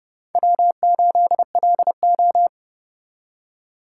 Morse code, heard as a keyed tone, about 28 wpm, 700 Hz. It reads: W8LO